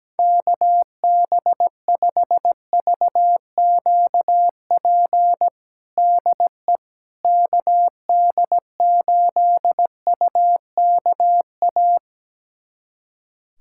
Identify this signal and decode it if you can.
Morse code, heard as a keyed tone, about 17 wpm, 700 Hz. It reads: KB5VQP DE KD8UKA
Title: USA Callsign DE Callsign - 6 Characters - 17wpm